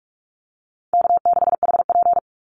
Morse code, heard as a keyed tone, about 44 wpm, 700 Hz. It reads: K6HP
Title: Call Signs Repeated 2x - 45wpm